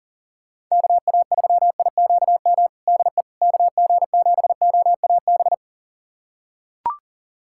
Morse code, heard as a keyed tone, about 40 wpm, 700 Hz. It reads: KA3IQM DE KG7OAB
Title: USA Callsign DE Callsign - 6 Characters - 40wpm